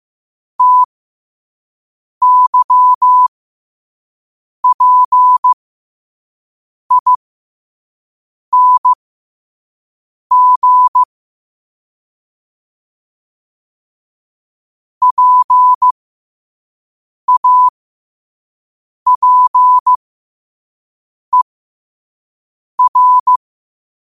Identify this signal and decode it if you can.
Morse code, heard as a keyed tone, about 15 wpm, spaced as for 5 wpm, 1000 Hz. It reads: TYPING PAPER